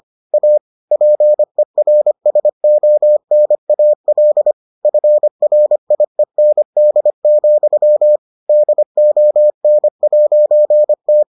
Morse code, heard as a keyed tone, about 25 wpm, 600 Hz. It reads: A PERSONAL FRIEND, DON'T